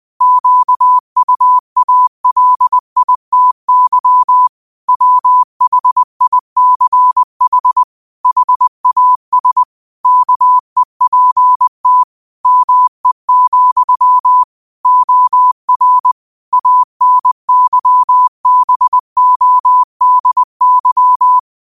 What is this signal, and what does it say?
Morse code, heard as a keyed tone, about 20 wpm, 1000 Hz. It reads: QUALITY WHICH HAS KEPT ME, OR ANYBODY
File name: qotdfiles / 20wpm